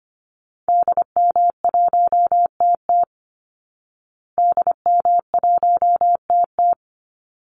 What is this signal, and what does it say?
Morse code, heard as a keyed tone, about 25 wpm, 700 Hz. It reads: DM1TT DM1TT
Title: Call Signs Repeated 3x - 25wpm